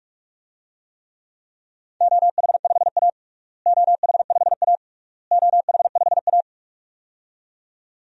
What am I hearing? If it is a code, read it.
Morse code, heard as a keyed tone, about 41 wpm, 700 Hz. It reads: OH5A OH5A OH5A
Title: DXCC-100 Call Signs - 4 Characters - Repeated 3x - 45wpm